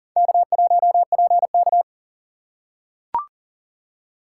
Morse code, heard as a keyed tone, about 40 wpm, 700 Hz. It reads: K1PK
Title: USA Call Signs - 4 Characters - 4x Character Spacing - 40wpm